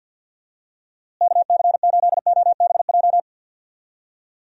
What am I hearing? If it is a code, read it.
Morse code, heard as a keyed tone, about 50 wpm, 700 Hz. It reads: KC8OBJ
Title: USA Call Signs - 6 Characters - Repeated 2x - 50wpm